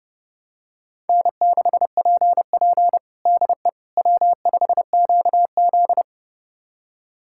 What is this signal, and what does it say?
Morse code, heard as a keyed tone, about 30 wpm, 700 Hz. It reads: N6PP DE W5QZ
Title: USA Callsign DE Callsign - 4 Characters - 30wpm